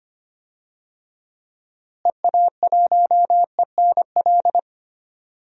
Morse code, heard as a keyed tone, about 25 wpm, 700 Hz. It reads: EA1ENL
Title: DXCC-100 Call Signs - 6 Characters - Repeated 3x - 25wpm